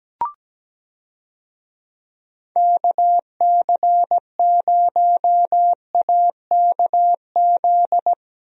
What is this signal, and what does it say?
Morse code, heard as a keyed tone, about 17 wpm, 700 Hz. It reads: KC0AKZ
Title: USA Call Signs - 6 Characters - Repeated 2x - 17wpm